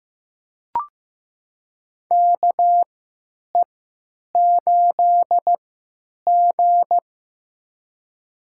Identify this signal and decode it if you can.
Morse code, heard as a keyed tone, about 15 wpm, spaced as for 9 wpm, 700 Hz. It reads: KE8G
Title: USA Call Signs - 4 Characters - 3x Character Spacing - 15wpm